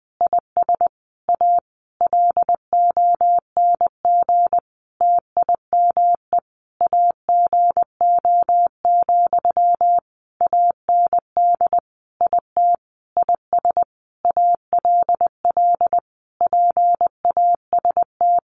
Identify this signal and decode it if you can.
Morse code, heard as a keyed tone, about 19 wpm, 700 Hz. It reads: IS A LONG TIME AGO, AND IT IS ALL PAST
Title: Sentences from Top 600 Words 20wpm